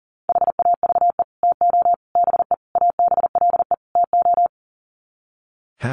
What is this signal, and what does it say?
Morse code, heard as a keyed tone, about 40 wpm, 700 Hz. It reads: HAVE TO BE ABLE TO
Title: Sets of 5 Words 40wpm